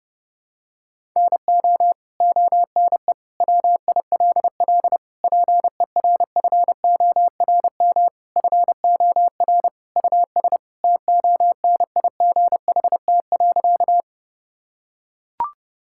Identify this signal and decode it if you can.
Morse code, heard as a keyed tone, about 30 wpm, 700 Hz. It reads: NO ONE WILL PERFORM FOR US TONIGHT.
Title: Sentences from Top 1000 Words - Review Part 5 - 30wpm